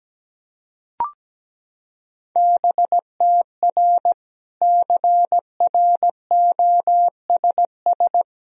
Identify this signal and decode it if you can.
Morse code, heard as a keyed tone, about 17 wpm, 700 Hz. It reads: BTR CROSS
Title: Sentences from Top 400 Words and Abbreviations 17wpm